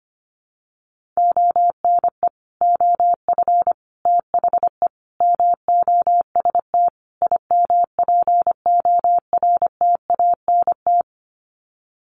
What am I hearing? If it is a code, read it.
Morse code, heard as a keyed tone, about 25 wpm, 700 Hz. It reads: ONE OF THE MOST IMPORTANT
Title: Sets of 5 Words 25wpm